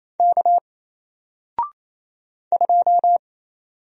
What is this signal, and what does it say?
Morse code, heard as a keyed tone, about 27 wpm, 700 Hz. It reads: K 2
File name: Single-Letter-Number-28wpm